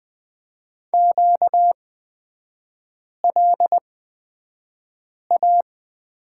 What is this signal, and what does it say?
Morse code, heard as a keyed tone, about 20 wpm, spaced as for 5 wpm, 690 Hz. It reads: QLA
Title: 236 Lesson 38e - Letter q - Three Characters 20wpm